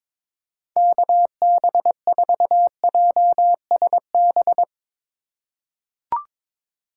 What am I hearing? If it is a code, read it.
Morse code, heard as a keyed tone, about 22 wpm, 700 Hz. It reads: KB4JSB
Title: USA Call Signs - 6 Characters - 4x Character Spacing - 22wpm